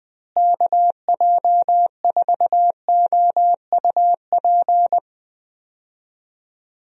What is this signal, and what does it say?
Morse code, heard as a keyed tone, about 20 wpm, 700 Hz. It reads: KJ4OUP
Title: USA Call Signs - 6 Characters - Repeated 3x - 20wpm